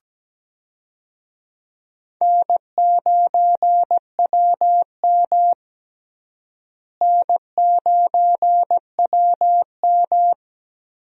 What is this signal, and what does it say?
Morse code, heard as a keyed tone, about 17 wpm, 700 Hz. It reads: N9WM N9WM
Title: USA Call Signs - 4 Characters - Repeated 2x - 17wpm